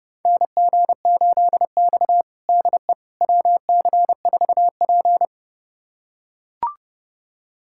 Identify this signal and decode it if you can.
Morse code, heard as a keyed tone, about 29 wpm, 700 Hz. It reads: NG8X DE WC4P
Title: USA Callsign DE Callsign - 4 Characters - 30wpm